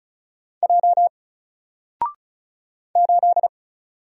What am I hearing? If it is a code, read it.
Morse code, heard as a keyed tone, about 36 wpm, 700 Hz. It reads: J 8
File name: Single-Letter-Number-35wpm